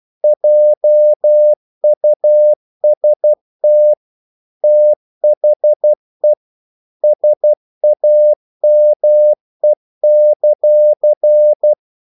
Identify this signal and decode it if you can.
Morse code, heard as a keyed tone, about 12 wpm, 600 Hz. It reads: JUST THE SAME;
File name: CW / Practice / If12